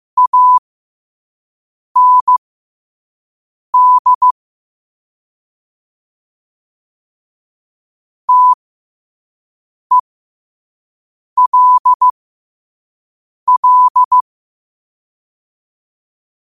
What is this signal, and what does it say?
Morse code, heard as a keyed tone, about 15 wpm, spaced as for 5 wpm, 1000 Hz. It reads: AND TELL
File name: qotdfiles / 5wpm